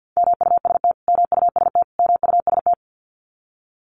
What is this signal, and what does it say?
Morse code, heard as a keyed tone, about 50 wpm, 700 Hz. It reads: MUST MUST MUST
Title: Top 500 Words Repeated 50wpm